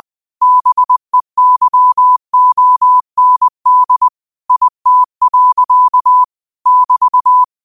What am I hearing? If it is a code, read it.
Morse code, heard as a keyed tone, about 20 wpm, 1000 Hz. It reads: BEYOND IT. <BT>
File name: qotdfiles / 20wpm